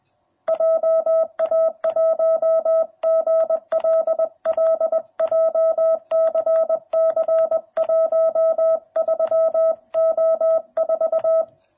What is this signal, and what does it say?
Morse code, heard as a keyed tone, about 20 wpm, 660 Hz. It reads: JA1GLL JCC 13O4